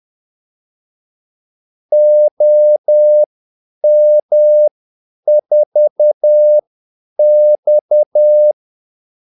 Morse code, heard as a keyed tone, about 10 wpm, spaced as for 8 wpm, 600 Hz. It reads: OM4X
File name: Callsigns-10-8wpm